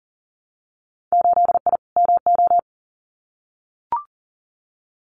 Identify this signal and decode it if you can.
Morse code, heard as a keyed tone, about 40 wpm, 700 Hz. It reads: 8I MO
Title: Winter Field Day Exchanges Encore 40wpm